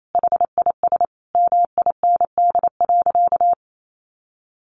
Morse code, heard as a keyed tone, about 28 wpm, 700 Hz. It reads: HIS MIND.
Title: Sentences from Top 1400 Words 28wpm